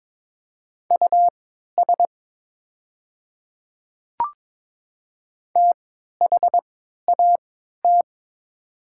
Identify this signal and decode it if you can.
Morse code, heard as a keyed tone, about 22 wpm, spaced as for 12 wpm, 700 Hz. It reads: US THAT
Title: 3x Character Spacing - Top 200 Words - 22wpm